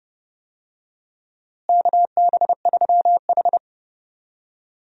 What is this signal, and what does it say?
Morse code, heard as a keyed tone, about 30 wpm, 700 Hz. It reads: KB3H
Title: USA Call Signs - 4 Characters - Repeated 3x - 30wpm